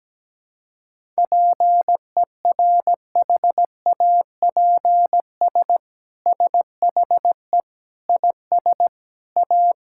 Morse code, heard as a keyed tone, about 17 wpm, 700 Hz. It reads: PERHAPS SHE IS A
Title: Sentences from Top 1000 Words - Review Part 5 - 17wpm